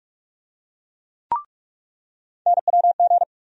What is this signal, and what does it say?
Morse code, heard as a keyed tone, about 46 wpm, 700 Hz. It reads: NWG
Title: USA Call Sign Postfixes - 3 Letters 45wpm